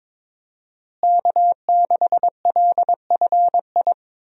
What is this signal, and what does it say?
Morse code, heard as a keyed tone, about 22 wpm, 700 Hz. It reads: K6LFI
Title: USA Call Signs - 5 Characters - Repeated 3x - 22wpm